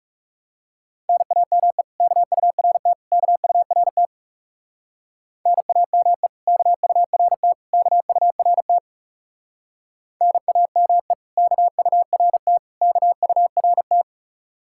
Morse code, heard as a keyed tone, about 37 wpm, 700 Hz. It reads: NAME KURT KURT NAME KURT KURT NAME KURT KURT
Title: Speed-Racing Common QSO Phrases Encore - 30wpm